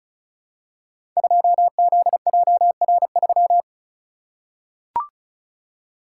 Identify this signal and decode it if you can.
Morse code, heard as a keyed tone, about 36 wpm, 700 Hz. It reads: 2ZJR3
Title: Code Groups - 5 Letter:Numbers - 35wpm